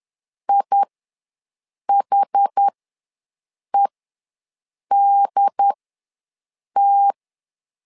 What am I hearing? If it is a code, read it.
Morse code, heard as a keyed tone, about 10 wpm, spaced as for 6 wpm, 800 Hz. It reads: IHEDT
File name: LEZIONI / 2 UHDN / 2 UHD ws 10wpm 2058